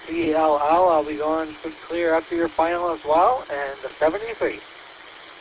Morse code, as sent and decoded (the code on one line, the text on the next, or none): none